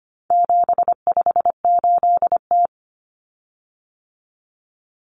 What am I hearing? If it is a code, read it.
Morse code, heard as a keyed tone, about 25 wpm, 700 Hz. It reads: 758T
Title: Common Cut Numbers - 4-Digit 25wpm